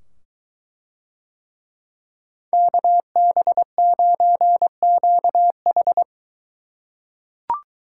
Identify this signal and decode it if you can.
Morse code, heard as a keyed tone, about 22 wpm, 700 Hz. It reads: KB9QH